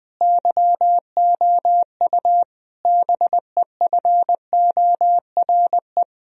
Morse code, heard as a keyed tone, about 20 wpm, 700 Hz. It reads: YOU BEFORE
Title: Sentences from Top 1000 Words - Review Part 2 - 20wpm